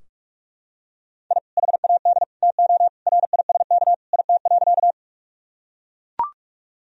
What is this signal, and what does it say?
Morse code, heard as a keyed tone, about 41 wpm, 700 Hz. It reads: I HAD TO RISK IT.
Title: Sentences from Top 1000 Words - Review Part 4 - 45wpm